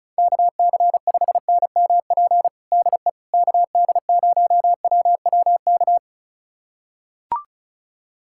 Morse code, heard as a keyed tone, about 36 wpm, 700 Hz. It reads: KC5NMP DE KD0WWK
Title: USA Callsign DE Callsign 35wpm